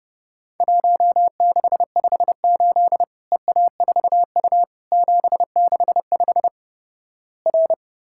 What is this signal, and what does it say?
Morse code, heard as a keyed tone, about 31 wpm, 700 Hz. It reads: 1658 EA4U 765 R
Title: Worked All Europe DX Contest - QTC Exchange - 30wpm